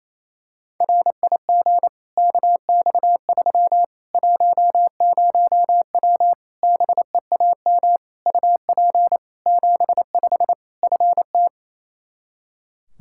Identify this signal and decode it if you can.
Morse code, heard as a keyed tone, about 27 wpm, 700 Hz. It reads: RIG KX3 10W BEAM UP 75 FT
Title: Practice QSO - Exchange 2 - Info - 28wpm